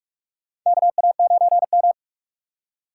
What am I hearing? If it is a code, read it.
Morse code, heard as a keyed tone, about 43 wpm, 700 Hz. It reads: KA9M
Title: USA Call Signs - 4 Characters - 3x Character Spacing - 45wpm